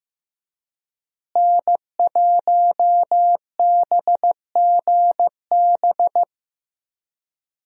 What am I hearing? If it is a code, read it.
Morse code, heard as a keyed tone, about 15 wpm, 700 Hz. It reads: N1BGB
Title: USA Call Signs - 5 Characters - Repeated 2x - 15wpm